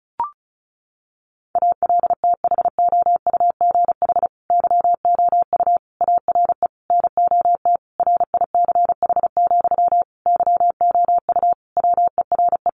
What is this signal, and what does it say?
Morse code, heard as a keyed tone, about 34 wpm, 700 Hz. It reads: ALTHOUGH YOU ARE NOT RICH, YOU WERE
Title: Sentences from Top 1100 Words 35wpm